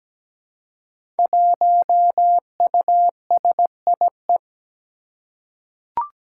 Morse code, heard as a keyed tone, about 17 wpm, 700 Hz. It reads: 1USIE
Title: Five Challenging Letter-Number Combinations 17wpm